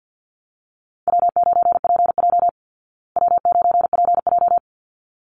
Morse code, heard as a keyed tone, about 50 wpm, 700 Hz. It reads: W9PJ W9PJ
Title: Call Signs Repeated 3x - 50wpm